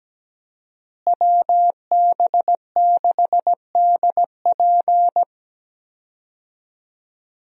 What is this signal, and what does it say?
Morse code, heard as a keyed tone, about 17 wpm, 700 Hz. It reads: WB6DP
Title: USA Call Signs - 5 Characters - Repeated 3x - 17wpm